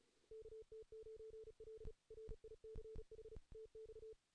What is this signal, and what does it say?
Morse code, heard as a keyed tone, about 37 wpm, 440 Hz. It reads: KT9L RICH TX
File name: 2016 JAN CW NAQP / 2016 Jan naqp k5ot - 0-3